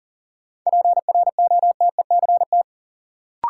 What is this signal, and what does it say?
Morse code, heard as a keyed tone, about 39 wpm, 700 Hz. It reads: PROTECT
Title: Sentences from Top 900 Words - New Word Drill - 40wpm